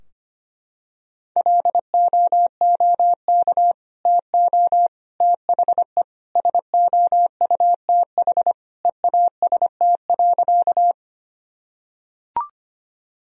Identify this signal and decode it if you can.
Morse code, heard as a keyed tone, about 25 wpm, 700 Hz. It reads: LOOK TO THE SOUTH EAST.
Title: Sentences from Top 1000 Words - Review Part 3 - 25wpm